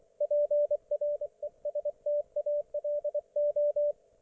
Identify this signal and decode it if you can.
Morse code, heard as a keyed tone, about 24 wpm, 590 Hz, with radio noise. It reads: PRESTALO
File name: Archiv / CUC212 / ok1fhi cul212